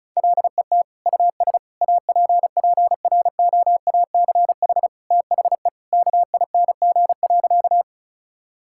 Morse code, heard as a keyed tone, about 34 wpm, 700 Hz. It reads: LET US APPROACH THE KING.
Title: Sentences from Top 1000 Words - Review Part 6 - 35wpm